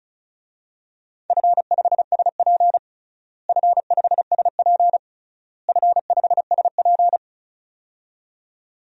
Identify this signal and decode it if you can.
Morse code, heard as a keyed tone, about 36 wpm, 700 Hz. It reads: F5SP F5SP F5SP
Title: DXCC-100 Call Signs - 4 Characters - Repeated 3x - 35wpm